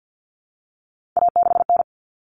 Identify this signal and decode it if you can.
Morse code, heard as a keyed tone, about 50 wpm, 700 Hz. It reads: A6N
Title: All Cut Numbers - 3-Digit 50wpm